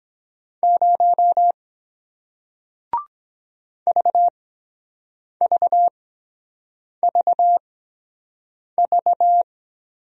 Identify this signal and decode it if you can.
Morse code, heard as a keyed tone, about 23 wpm, spaced as for 7 wpm, 700 Hz. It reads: 0 VVVV